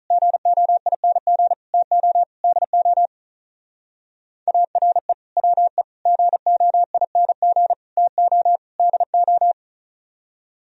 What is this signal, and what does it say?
Morse code, heard as a keyed tone, about 43 wpm, 700 Hz. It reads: GOING TO DO ARE WE GOING TO DO
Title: Speed-Racing - Sets of 5 Words 35wpm